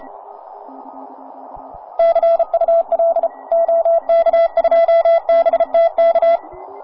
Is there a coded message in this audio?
Yes: Morse code from 2.0 to 6.4 s, heard as a keyed tone, about 30 wpm, 690 Hz. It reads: CUL OK2BTK